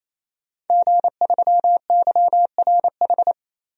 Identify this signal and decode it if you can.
Morse code, heard as a keyed tone, about 28 wpm, 700 Hz. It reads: G3YRH
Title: DXCC-100 Call Signs - 5 Characters - 28wpm